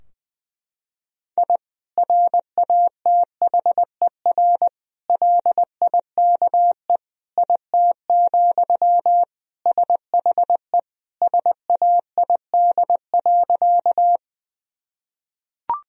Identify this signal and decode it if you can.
Morse code, heard as a keyed tone, about 20 wpm, 700 Hz. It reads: I RATHER LIKE IT, SHE SAID.
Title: Sentences from Top 1000 Words - Review Part 4 - 20wpm